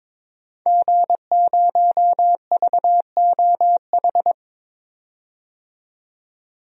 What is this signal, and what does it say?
Morse code, heard as a keyed tone, about 22 wpm, 700 Hz. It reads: G0VOH
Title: DXCC-100 Call Signs - 5 Characters - 22wpm